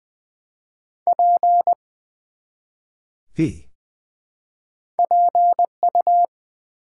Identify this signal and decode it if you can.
Morse code, heard as a keyed tone, about 20 wpm, 700 Hz. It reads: P PU
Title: Build A Word - 1 Letter At A Time - 3-Letter Words - 20wpm